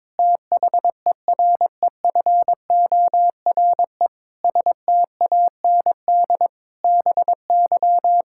Morse code, heard as a keyed tone, about 22 wpm, 700 Hz. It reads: THEREFORE STAND BY
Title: Sentences from Top 1000 Words - Review Part 5 - 22wpm